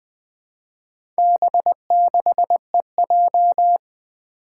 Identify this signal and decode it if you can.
Morse code, heard as a keyed tone, about 20 wpm, 700 Hz. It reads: B6EJ
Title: Four Challenging Letter-Number Combinations 20wpm